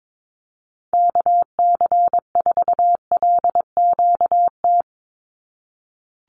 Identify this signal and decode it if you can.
Morse code, heard as a keyed tone, about 22 wpm, 700 Hz. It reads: KC4LQT
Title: USA Call Signs 22wpm